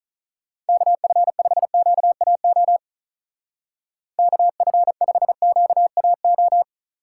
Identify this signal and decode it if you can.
Morse code, heard as a keyed tone, about 43 wpm, 700 Hz. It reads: KF5QAO KF5QAO
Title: Speed Racing - USA 2x3 Call Signs 35wpm